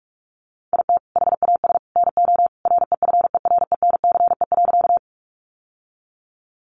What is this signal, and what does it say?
Morse code, heard as a keyed tone, about 44 wpm, 700 Hz. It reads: IT HAS NO REFERENCE.
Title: Sentences from Top 1200 Words 45wpm